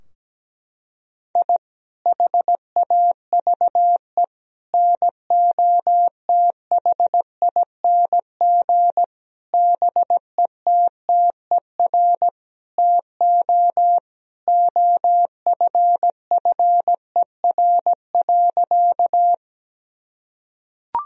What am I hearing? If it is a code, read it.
Morse code, heard as a keyed tone, about 17 wpm, 700 Hz. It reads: I HAVE NOTHING BETTER TO OFFER.
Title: Sentences from Top 1000 Words - Review Part 4 - 17wpm